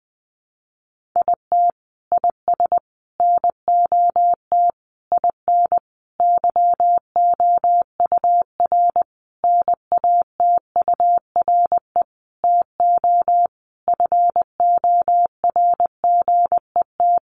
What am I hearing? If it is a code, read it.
Morse code, heard as a keyed tone, about 20 wpm, 700 Hz. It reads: IT IS NOT IN YOUR NATURE TO FORGET
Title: Sentences from Top 1200 Words 20wpm